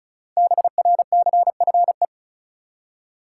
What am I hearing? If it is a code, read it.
Morse code, heard as a keyed tone, about 36 wpm, 700 Hz. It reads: BRCFE
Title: Code Groups - 5 Letters - 35wpm